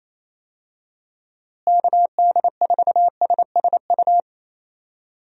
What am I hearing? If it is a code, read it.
Morse code, heard as a keyed tone, about 28 wpm, 700 Hz. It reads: KD4SSU
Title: USA Call Signs - 6 Characters - Repeated 3x - 28wpm